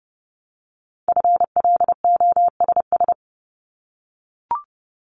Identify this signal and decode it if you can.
Morse code, heard as a keyed tone, about 30 wpm, 700 Hz. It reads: FLOSS
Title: Multi-letter Phonemes 30wpm